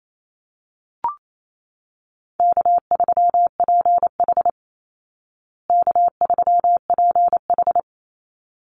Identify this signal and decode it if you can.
Morse code, heard as a keyed tone, about 28 wpm, 700 Hz. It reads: K3PH K3PH
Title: Easy Call Signs Repeated 2x - 28wpm